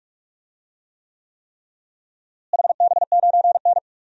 Morse code, heard as a keyed tone, about 41 wpm, 700 Hz. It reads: HB9N
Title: DXCC-100 Call Signs - 4 Characters - Repeated 3x - 45wpm